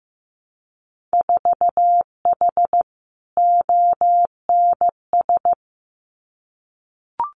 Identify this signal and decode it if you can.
Morse code, heard as a keyed tone, about 15 wpm, 700 Hz. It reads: 4H ONS
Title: Winter Field Day Exchanges Encore 15wpm